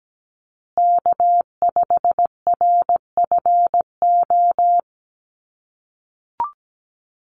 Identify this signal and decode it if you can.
Morse code, heard as a keyed tone, about 17 wpm, 700 Hz. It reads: K5RFO
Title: USA Call Signs Repeated 2x 17wpm